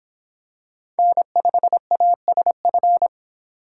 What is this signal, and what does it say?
Morse code, heard as a keyed tone, about 26 wpm, 700 Hz. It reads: N5ASF